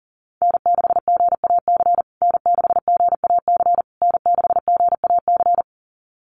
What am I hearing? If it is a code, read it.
Morse code, heard as a keyed tone, about 40 wpm, 700 Hz. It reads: N6GAC N6GAC N6GAC
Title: USA Call Signs Repeated 3x 40wpm